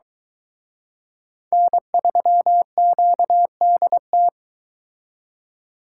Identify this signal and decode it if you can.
Morse code, heard as a keyed tone, about 23 wpm, 700 Hz. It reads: N3QDT